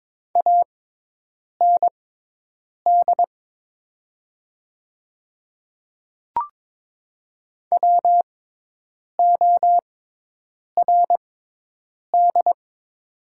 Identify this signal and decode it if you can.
Morse code, heard as a keyed tone, about 22 wpm, spaced as for 8 wpm, 700 Hz. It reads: AND WORD
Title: 6x Character Spacing - Top 100 Words - 22wpm